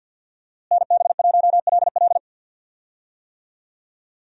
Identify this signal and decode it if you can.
Morse code, heard as a keyed tone, about 52 wpm, 700 Hz. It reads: NB1LL
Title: USA Call Signs - 5 Characters - Repeated 3x - 50wpm